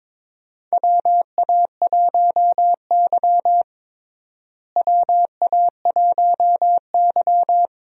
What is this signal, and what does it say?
Morse code, heard as a keyed tone, about 22 wpm, 700 Hz. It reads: WA1Y WA1Y
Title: USA Call Signs - 4 Characters - Repeated 2x - 22wpm